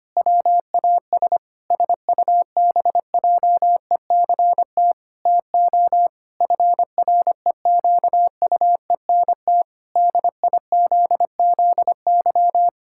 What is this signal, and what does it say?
Morse code, heard as a keyed tone, about 25 wpm, 700 Hz. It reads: WAS SUBJECT TO FREQUENT DIZZY
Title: Pangram Challenge v2 - 25wpm